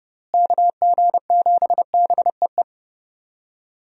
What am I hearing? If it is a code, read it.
Morse code, heard as a keyed tone, about 29 wpm, 700 Hz. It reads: KG7BEE
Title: USA Call Signs - 6 Characters - Repeated 3x - 30wpm